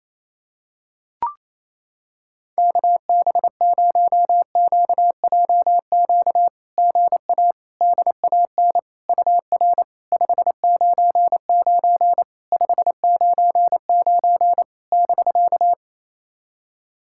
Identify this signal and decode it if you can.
Morse code, heard as a keyed tone, about 28 wpm, 690 Hz. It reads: KB0QJQ GA DAN UR 599 599 <BK>
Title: SOTA QSO Practice 28wpm